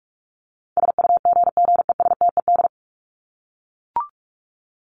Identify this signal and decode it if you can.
Morse code, heard as a keyed tone, about 44 wpm, 700 Hz. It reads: SUGGESTED
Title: ED Suffix - Top 100 Words with Base Words 45wpm